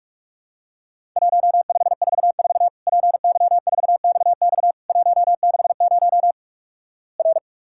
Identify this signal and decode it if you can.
Morse code, heard as a keyed tone, about 48 wpm, 700 Hz. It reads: 1544 PY4XX 160 R
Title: Worked All Europe DX Contest - QTC Exchange - 45wpm